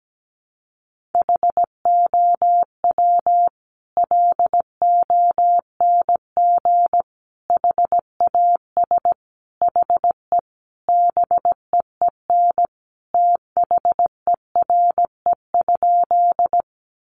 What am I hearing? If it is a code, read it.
Morse code, heard as a keyed tone, about 17 wpm, 700 Hz. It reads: HOW LONG HAS HE BEEN THERE?
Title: Sentences from Top 100 Words 17wpm